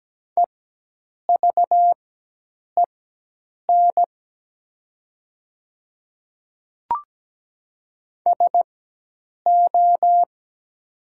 Morse code, heard as a keyed tone, about 17 wpm, spaced as for 8 wpm, 700 Hz. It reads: EVEN SO
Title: 4x Character Spacing - Top 200 Words - 17wpm